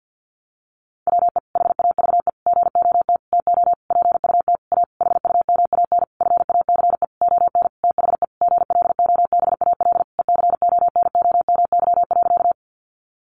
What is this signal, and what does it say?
Morse code, heard as a keyed tone, about 50 wpm, 700 Hz. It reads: WE HAVE GOT TO PUT A HUMAN FACE ON THE GLOBAL ECONOMY.
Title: Sentences from Top 1200 Words 50wpm